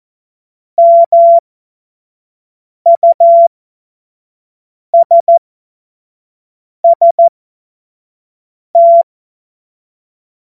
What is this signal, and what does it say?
Morse code, heard as a keyed tone, about 14 wpm, spaced as for 5 wpm, 680 Hz. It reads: MUSST
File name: Morsekurs / Witze / Witz 10.Oktober 2023 25 F70